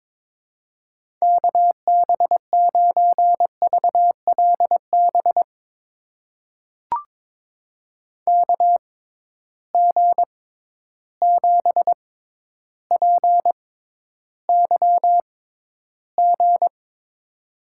Morse code, heard as a keyed tone, about 22 wpm, spaced as for 11 wpm, 700 Hz. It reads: KB9VLB KG7PYG
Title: USA Call Signs - 6 Characters - 6x Character Spacing - 22wpm